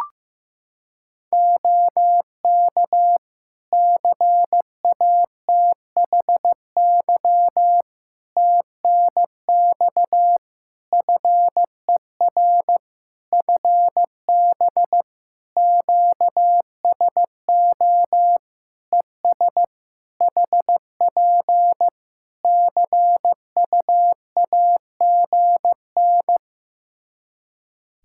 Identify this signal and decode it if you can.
Morse code, heard as a keyed tone, about 16 wpm, 700 Hz. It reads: OK CATHY TNX FER FB QSO ES HP CUAGN
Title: Practice QSO - Exchange 4 - Ending - 15wpm